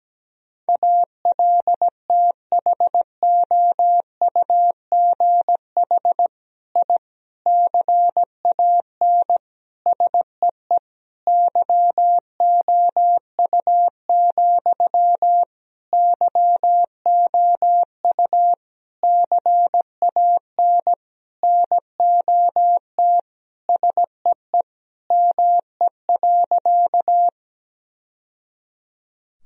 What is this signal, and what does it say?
Morse code, heard as a keyed tone, about 17 wpm, 700 Hz. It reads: ALTHOUGH I CAN SEE YOU, YOU CAN NOT SEE ME.
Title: Sentences from Top 1000 Words - Review Part 3 - 17wpm